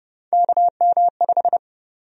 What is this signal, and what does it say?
Morse code, heard as a keyed tone, about 30 wpm, 700 Hz. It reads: KM5
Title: USA Call Sign Prefixes - 30wpm